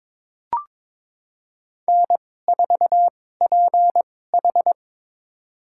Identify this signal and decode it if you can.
Morse code, heard as a keyed tone, about 23 wpm, spaced as for 16 wpm, 700 Hz. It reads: N4PH
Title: USA Call Signs - 4 Characters - 2x Character Spacing - 22wpm